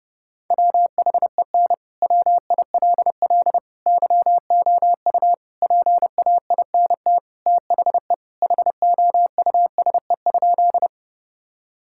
Morse code, heard as a keyed tone, about 30 wpm, 700 Hz. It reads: WHEN WILL YOU PAINT THE HOUSE?
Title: Sentences from Top 1000 Words - Review Part 6 - 30wpm